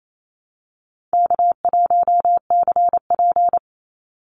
Morse code, heard as a keyed tone, about 28 wpm, 700 Hz. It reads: K1CP
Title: Easy Call Signs Repeated 3x - 28wpm